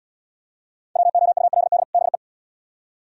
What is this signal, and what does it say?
Morse code, heard as a keyed tone, about 77 wpm, 700 Hz. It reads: WOULD BE